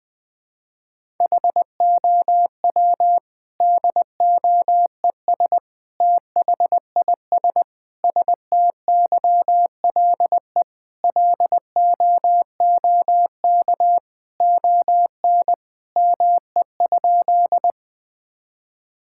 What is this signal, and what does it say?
Morse code, heard as a keyed tone, about 20 wpm, 700 Hz. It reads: HOW DOES THIS STYLE LOOK ON ME?
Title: Sentences from Top 1000 Words - Review Part 5 - 20wpm